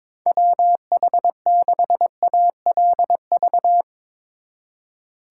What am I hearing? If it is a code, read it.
Morse code, heard as a keyed tone, about 24 wpm, 700 Hz. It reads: WH6ALV
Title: Speed Racing - USA 2x3 Call Signs 22wpm